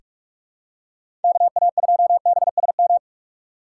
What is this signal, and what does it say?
Morse code, heard as a keyed tone, about 44 wpm, 700 Hz. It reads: KA2BSM
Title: USA Call Signs - 2x Character Spacing - 45wpm